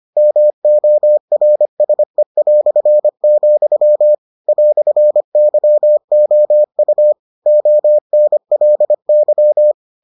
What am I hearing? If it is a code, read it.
Morse code, heard as a keyed tone, about 25 wpm, 600 Hz. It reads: MORSE", "YOU ONLY